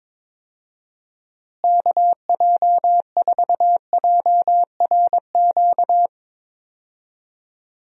Morse code, heard as a keyed tone, about 22 wpm, 700 Hz. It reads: KJ4JRQ
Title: USA Call Signs - 6 Characters - Repeated 2x - 22wpm